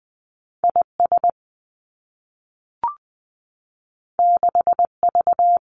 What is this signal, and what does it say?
Morse code, heard as a keyed tone, about 20 wpm, 700 Hz. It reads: I S 6 V
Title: Two Challenging Letter-Number Combinations 20wpm